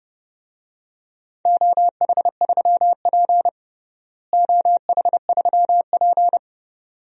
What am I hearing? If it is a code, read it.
Morse code, heard as a keyed tone, about 30 wpm, 700 Hz. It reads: OH3P OH3P
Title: DXCC-100 Call Signs - 4 Characters - Repeated 2x - 30wpm